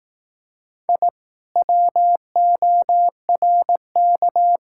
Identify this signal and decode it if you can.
Morse code, heard as a keyed tone, about 18 wpm, 700 Hz. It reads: I WORK
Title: Speed-Racing Common QSO Phrases Encore - 15wpm